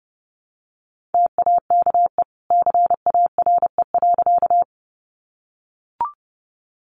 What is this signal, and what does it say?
Morse code, heard as a keyed tone, about 30 wpm, 700 Hz. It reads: TAKE CARE.
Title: Sentences from Top 300 Words 30wpm